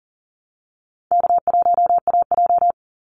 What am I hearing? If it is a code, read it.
Morse code, heard as a keyed tone, about 40 wpm, 700 Hz. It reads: K1AJ
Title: Easy Call Signs Repeated 2x - 40wpm